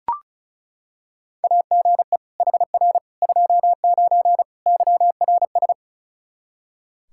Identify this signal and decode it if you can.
Morse code, heard as a keyed tone, about 36 wpm, 700 Hz. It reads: AGE HR 29 YRS
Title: Practice QSO - Exchange 3 - More Info - 35wpm